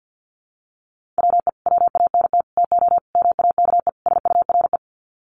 Morse code, heard as a keyed tone, about 50 wpm, 700 Hz. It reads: WE WANT TO MAKE SURE
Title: Sets of 5 Words Encore2 50wpm